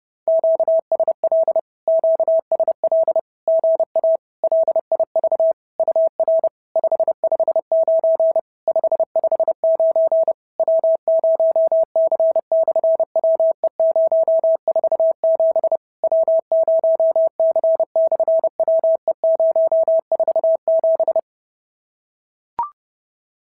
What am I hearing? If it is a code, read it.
Morse code, heard as a keyed tone, about 32 wpm, 690 Hz. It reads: QSL QSL GA LIV UR 559 559 W0C/WE047 W0C/WE047
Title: SOTA QSO Practice 30wpm